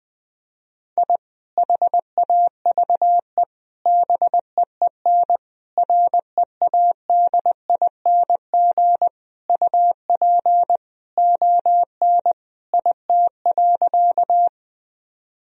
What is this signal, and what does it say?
Morse code, heard as a keyed tone, about 20 wpm, 700 Hz. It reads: I HAVE BEEN READING UP ON IT.
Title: Sentences from Top 1000 Words - Review Part 3 - 20wpm